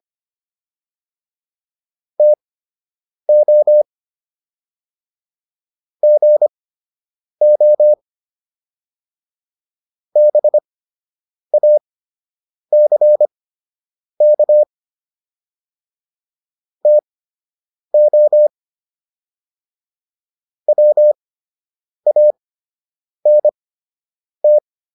Morse code, heard as a keyed tone, about 25 wpm, spaced as for 8 wpm, 600 Hz. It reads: TO GO BACK TO WANT